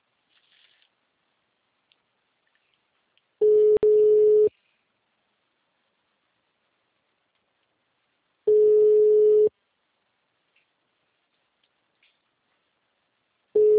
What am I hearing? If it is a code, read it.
Ringback tone (Europe, 425 Hz).